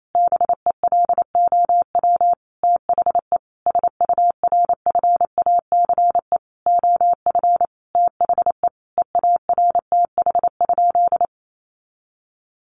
Morse code, heard as a keyed tone, about 28 wpm, 700 Hz. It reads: BELOW THE SURFACE OF THE EARTH?
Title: Sentences from Top 1200 Words 28wpm